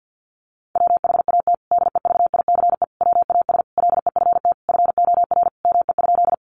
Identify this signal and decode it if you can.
Morse code, heard as a keyed tone, about 48 wpm, 700 Hz. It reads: WHAT DEVICE WAS LEFT FOR ME?
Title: Sentences from Top 1500 Words 50wpm